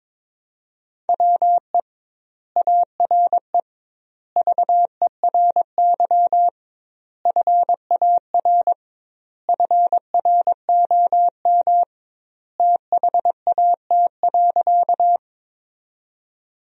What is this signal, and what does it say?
Morse code, heard as a keyed tone, about 23 wpm, 700 Hz. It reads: WE ARE VERY FAR FROM THAT.
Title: 2x Word-Spacing - Sentences from Top 300 Words - 22wpm